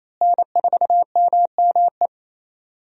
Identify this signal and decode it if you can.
Morse code, heard as a keyed tone, about 28 wpm, 700 Hz. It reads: N4MME
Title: USA Call Signs - 5 Characters - Repeated 3x - 28wpm